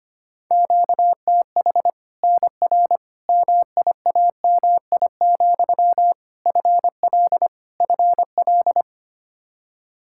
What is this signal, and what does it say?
Morse code, heard as a keyed tone, about 25 wpm, 700 Hz. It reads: QTH NR MIAMI, FL FL
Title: Practice QSO - Exchange 1 - Beginning - 25wpm